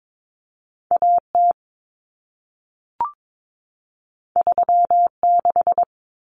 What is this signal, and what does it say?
Morse code, heard as a keyed tone, about 22 wpm, 700 Hz. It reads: AT 36
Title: Common Cut Numbers - 2-Digit 22wpm